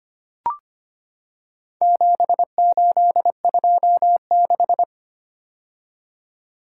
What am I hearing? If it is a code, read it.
Morse code, heard as a keyed tone, about 25 wpm, 700 Hz. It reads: 7826
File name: practice / 25wpm / Four Letter-Number Combinations - 25wpm